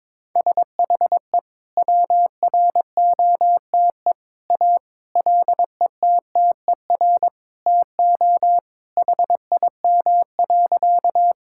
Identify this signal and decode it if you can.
Morse code, heard as a keyed tone, about 22 wpm, 700 Hz. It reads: SHE WROTE A LETTER TO HIM.
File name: practice / 22wpm / Sentences with 2-Word Phrases - 22wpm